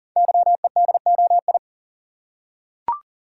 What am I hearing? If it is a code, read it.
Morse code, heard as a keyed tone, about 39 wpm, 690 Hz. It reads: YEDOI
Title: Code Groups - 5 Letters - 40wpm